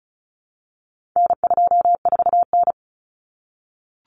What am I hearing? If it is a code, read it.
Morse code, heard as a keyed tone, about 34 wpm, 700 Hz. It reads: N24N
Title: Common Cut Numbers - 4-Digit 35wpm